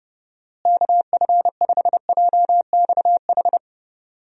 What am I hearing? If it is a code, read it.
Morse code, heard as a keyed tone, about 30 wpm, 700 Hz. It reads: KF5JXH
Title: USA Call Signs - 6 Characters - Repeated 2x - 30wpm